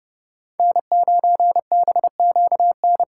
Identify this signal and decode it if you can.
Morse code, heard as a keyed tone, about 29 wpm, 700 Hz. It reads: N9BQN